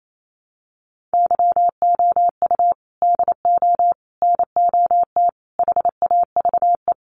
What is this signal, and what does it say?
Morse code, heard as a keyed tone, about 28 wpm, 700 Hz. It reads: YOU DO NOT HAVE
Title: Sets of 4 Words 28wpm